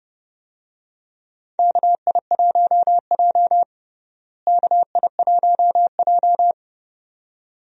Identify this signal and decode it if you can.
Morse code, heard as a keyed tone, about 30 wpm, 700 Hz. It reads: KI1J KI1J
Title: USA Call Signs - 4 Characters - Repeated 2x - 30wpm